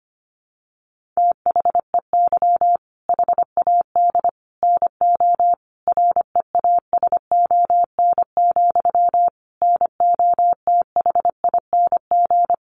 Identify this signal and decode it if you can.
Morse code, heard as a keyed tone, about 25 wpm, 700 Hz. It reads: THEY HAD NO REASON, NOTHING
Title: Sentences from Top 1500 Words 25wpm